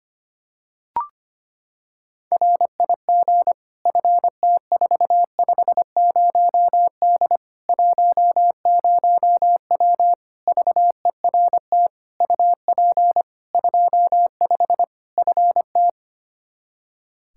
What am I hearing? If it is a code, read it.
Morse code, heard as a keyed tone, about 25 wpm, 700 Hz. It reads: RIG FT450D 10W VERT UP 25 FT
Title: Practice QSO - Exchange 2 - Info - 25wpm